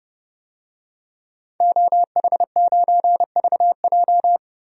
Morse code, heard as a keyed tone, about 32 wpm, 700 Hz. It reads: OH9VJ
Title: DXCC-100 Call Signs - 5 Characters - 30wpm